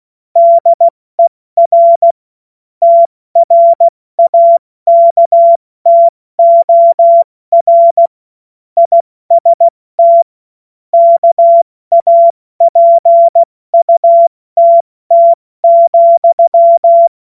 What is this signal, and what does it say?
Morse code, heard as a keyed tone, about 16 wpm, 680 Hz. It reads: DER TRAKTOR IST KAPUTT,
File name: Witz-08-Juli-2025-70-F80